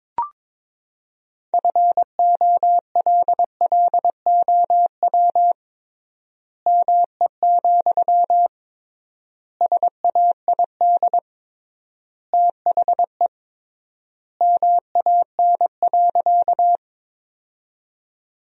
Morse code, heard as a keyed tone, about 22 wpm, 700 Hz. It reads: FOLLOW ME, SAID THE MAN.
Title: 3x Word-Spacing - Sentences from Top 200 Words - 22wpm